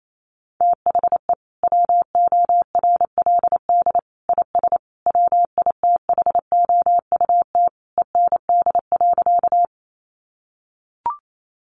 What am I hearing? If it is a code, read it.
Morse code, heard as a keyed tone, about 28 wpm, 700 Hz. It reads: THE WORLD IS WITHOUT END.
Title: Sentences from Top 600 Words 28wpm